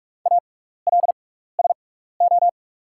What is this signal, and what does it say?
Morse code, heard as a keyed tone, about 44 wpm, spaced as for 15 wpm, 690 Hz. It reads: ALSO
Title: 6x Character Spacing - Top 300 Words - 45wpm